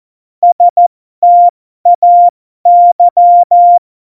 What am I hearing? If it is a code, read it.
Morse code, heard as a keyed tone, about 14 wpm, spaced as for 12 wpm, 700 Hz. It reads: STAY
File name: Morsekurs / Witze